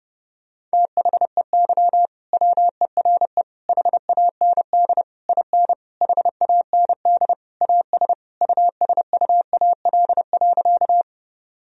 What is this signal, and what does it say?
Morse code, heard as a keyed tone, about 30 wpm, 700 Hz. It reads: THEY WERE HAND IN HAND AS USUAL.
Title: Sentences from Top 1000 Words - Review Part 2 - 30wpm